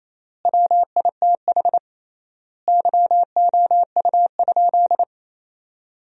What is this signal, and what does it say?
Morse code, heard as a keyed tone, about 28 wpm, 700 Hz. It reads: WITH YOU?
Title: 3x Word-Spacing - Sentences from Top 500 Words - 28wpm